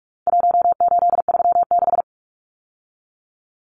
Morse code, heard as a keyed tone, about 44 wpm, 700 Hz. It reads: 1836
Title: Numbers - Contest Style - 4-Digit 45wpm